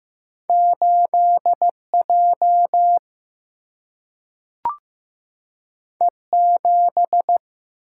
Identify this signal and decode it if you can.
Morse code, heard as a keyed tone, about 15 wpm, 700 Hz. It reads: 8 J E 7
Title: Single-Letter Single-Number 15wpm